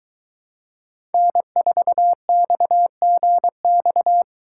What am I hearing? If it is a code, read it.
Morse code, heard as a keyed tone, about 23 wpm, 700 Hz. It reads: N4XGX